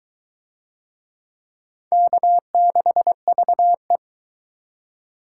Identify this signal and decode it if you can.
Morse code, heard as a keyed tone, about 23 wpm, 700 Hz. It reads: K6VE